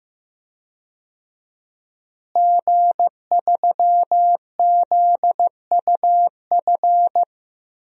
Morse code, heard as a keyed tone, about 15 wpm, 700 Hz. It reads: G3ZUF
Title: DXCC-100 Call Signs - 5 Characters - Repeated 2x - 15wpm